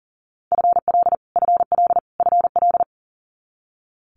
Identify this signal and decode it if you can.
Morse code, heard as a keyed tone, about 41 wpm, 700 Hz. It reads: FL FL FL
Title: Multi-letter Phonemes Repeated 3x 40wpm